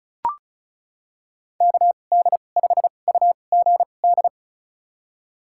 Morse code, heard as a keyed tone, about 36 wpm, spaced as for 25 wpm, 700 Hz. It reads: KD5UGD
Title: USA Call Signs - 6 Characters - 2x Character Spacing - 35wpm